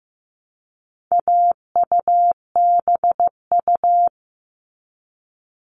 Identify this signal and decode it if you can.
Morse code, heard as a keyed tone, about 15 wpm, 700 Hz. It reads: AUBU
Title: All Cut Numbers - 4-Digit 15wpm